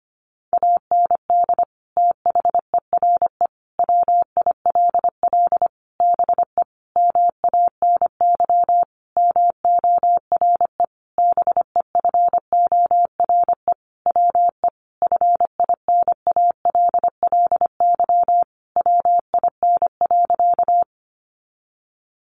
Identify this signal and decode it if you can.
Morse code, heard as a keyed tone, about 25 wpm, 700 Hz. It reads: AND THERE WILL BE MANY MORE BEFORE WE FINALLY WIN.
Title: Sentences from Top 1500 Words 25wpm